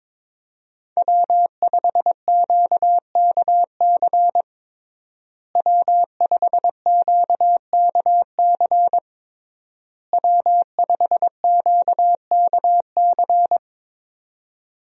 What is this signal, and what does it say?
Morse code, heard as a keyed tone, about 22 wpm, 700 Hz. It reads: W5QKC W5QKC W5QKC
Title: USA Call Signs - 5 Characters - Repeated 3x - 22wpm